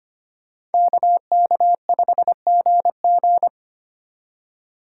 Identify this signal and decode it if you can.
Morse code, heard as a keyed tone, about 32 wpm, 700 Hz. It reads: KK5GG